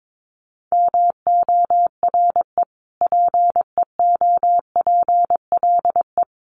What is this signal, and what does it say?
Morse code, heard as a keyed tone, about 22 wpm, 700 Hz. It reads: MORE PEOPLE
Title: Sentences from Top 1500 Words 22wpm